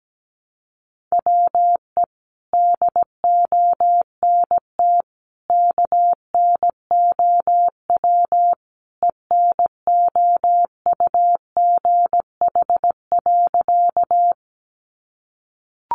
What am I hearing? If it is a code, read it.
Morse code, heard as a keyed tone, about 17 wpm, 700 Hz. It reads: WE DONT KNOW ENOUGH.
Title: Sentences from Top 400 Words 17wpm